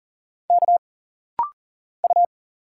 Morse code, heard as a keyed tone, about 40 wpm, 700 Hz. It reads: K U
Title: Single Letters - Rapid-Fire 40wpm